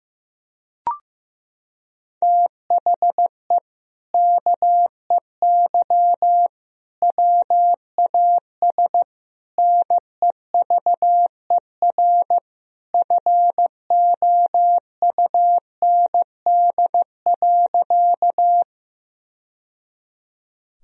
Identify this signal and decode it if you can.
Morse code, heard as a keyed tone, about 15 wpm, 700 Hz. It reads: THE KEY WAS NEVER FOUND.
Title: Sentences from Top 1000 Words - Review Part 4 - 15wpm